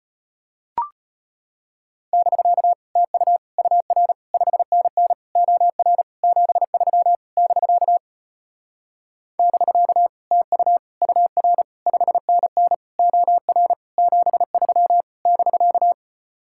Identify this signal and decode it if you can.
Morse code, heard as a keyed tone, about 33 wpm, 700 Hz. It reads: <BK> TU UR 5NN OR 73 <BK> <BK> TU UR 5NN OR 73 <BK>